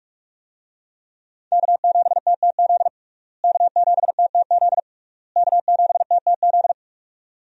Morse code, heard as a keyed tone, about 46 wpm, 700 Hz. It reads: K7TTZ K7TTZ K7TTZ
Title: USA Call Signs - 5 Characters - Repeated 3x - 45wpm